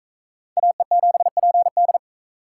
Morse code, heard as a keyed tone, about 38 wpm, 700 Hz. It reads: AE7PD